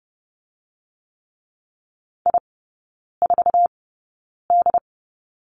Morse code, heard as a keyed tone, about 30 wpm, spaced as for 9 wpm, 700 Hz. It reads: I4D
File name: Lesson 15e - Number 5 - Three Characters 30wpm